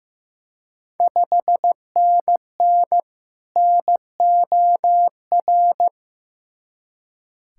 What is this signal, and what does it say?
Morse code, heard as a keyed tone, about 16 wpm, 700 Hz. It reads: 5NN NOR
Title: Ontario QSO Party Exchange 15wpm